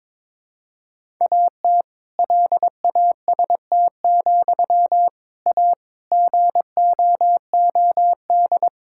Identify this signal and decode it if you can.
Morse code, heard as a keyed tone, about 22 wpm, 700 Hz. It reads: AT LAST, A GOOD
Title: Sentences from Top 1600 Words 22wpm